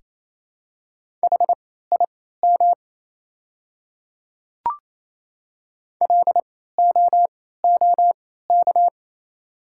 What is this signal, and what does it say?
Morse code, heard as a keyed tone, about 28 wpm, spaced as for 16 wpm, 690 Hz. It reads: HIM LOOK
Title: 3x Character Spacing - Top 100 Words - 28wpm